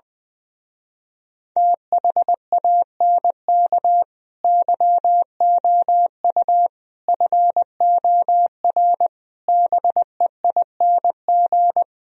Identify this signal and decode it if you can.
Morse code, heard as a keyed tone, about 21 wpm, 700 Hz. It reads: THANK YOU FOR BEING
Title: Speed-Racing - Sets of 5 Words 20wpm